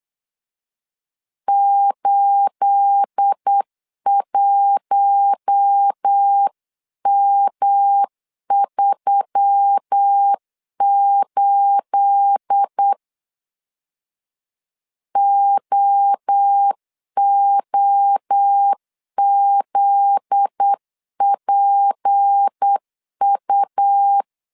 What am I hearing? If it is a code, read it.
Morse code, heard as a keyed tone, about 8 wpm, 800 Hz. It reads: 81M38 OOZPU
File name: CW / DET / 1 9 1500C S A 8wpm 5637